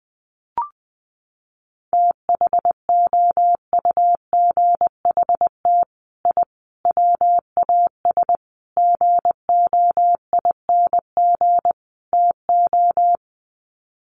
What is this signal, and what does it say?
Morse code, heard as a keyed tone, about 20 wpm, 700 Hz. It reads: THOUGHT I WAS GOING TO
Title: Sets of 5 Words Encore2 20wpm